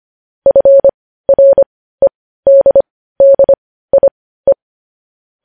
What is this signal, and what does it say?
Morse code, heard as a keyed tone, about 25 wpm, spaced as for 15 wpm, 550 Hz. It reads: FREDDIE
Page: News Headlines in Morse Code at 15 WPM